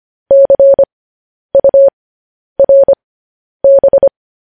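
Morse code, heard as a keyed tone, about 25 wpm, spaced as for 10 wpm, 550 Hz. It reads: CURB
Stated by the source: Morse code transcription of news headlines.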